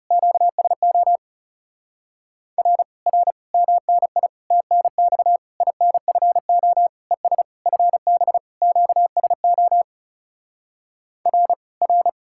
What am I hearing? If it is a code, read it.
Morse code, heard as a keyed tone, about 40 wpm, 700 Hz. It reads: QSO R R MNI TNX INFO ES FB QSO R R